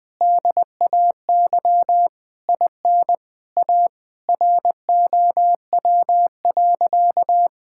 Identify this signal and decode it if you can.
Morse code, heard as a keyed tone, about 21 wpm, 700 Hz. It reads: DAY IN A ROW.
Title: Sentences with 7-Word Phrases - 20wpm